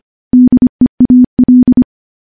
Morse code, heard as a keyed tone, about 25 wpm, 250 Hz. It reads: DEAL
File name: morse-training-25WPM